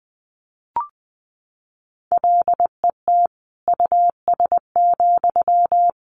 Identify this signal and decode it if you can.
Morse code, heard as a keyed tone, about 20 wpm, 700 Hz. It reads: LET US,
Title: Sentences from Top 1000 Words 20wpm